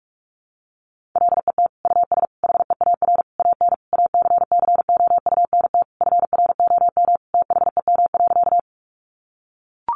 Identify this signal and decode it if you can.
Morse code, heard as a keyed tone, about 44 wpm, 700 Hz. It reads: LET US HEAR AN ACCOUNT FROM THEM.
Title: Sentences from Top 900 Words 45wpm